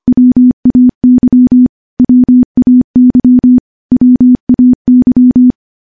Morse code, heard as a keyed tone, about 25 wpm, 250 Hz. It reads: WAY WAY WAY
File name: morse-training-25WPM